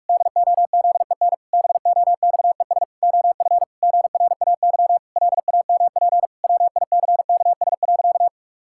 Morse code, heard as a keyed tone, about 48 wpm, 700 Hz. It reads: DOZEN BOXES OF GRAY LAMP WICKS.
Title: Pangram Challenge v2 - 45wpm